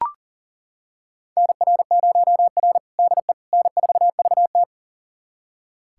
Morse code, heard as a keyed tone, about 40 wpm, 700 Hz. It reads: NR0R DE N4VT
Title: USA Callsign DE Callsign - 4 Characters - 40wpm